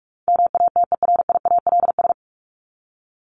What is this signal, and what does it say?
Morse code, heard as a keyed tone, about 44 wpm, 700 Hz. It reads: MATERIALS
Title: ES:S Suffix - Top 100 Words with Base Words 45wpm